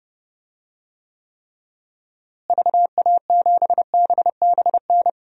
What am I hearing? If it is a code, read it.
Morse code, heard as a keyed tone, about 32 wpm, 700 Hz. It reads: VA7BBN
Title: DXCC-100 Call Signs - 6 Characters - Repeated 3x - 30wpm